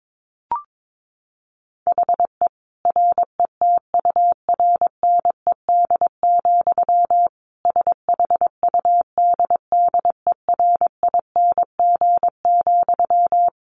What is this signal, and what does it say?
Morse code, heard as a keyed tone, about 22 wpm, 700 Hz. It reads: HE RETURNED, SHUDDERING,
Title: A Sound Of Thunder 22wpm